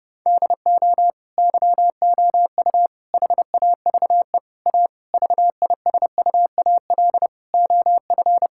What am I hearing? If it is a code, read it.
Morse code, heard as a keyed tone, about 30 wpm, 700 Hz. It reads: DO YOU HAVE A VISUAL OF
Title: Sentences from Top 1000 Words - Review Part 6 - 30wpm